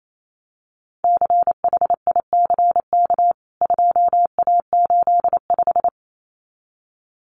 Morse code, heard as a keyed tone, about 28 wpm, 700 Hz. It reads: CHICK 2A85
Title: CWT Exchange 28wpm